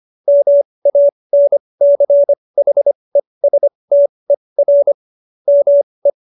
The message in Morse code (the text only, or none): MANCHESTER ME